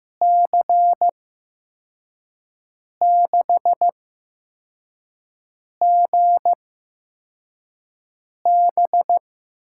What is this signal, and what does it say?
Morse code, heard as a keyed tone, about 15 wpm, spaced as for 4 wpm, 700 Hz. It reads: C6GB
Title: USA Call Signs - 8x Character Spacing - 15wpm